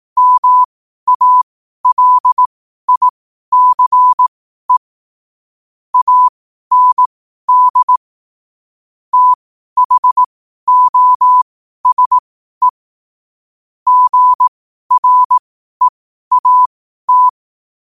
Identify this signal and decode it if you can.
Morse code, heard as a keyed tone, about 18 wpm, spaced as for 12 wpm, 1000 Hz. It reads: MALICE AND THOSE GREAT
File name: qotdfiles / 12wpm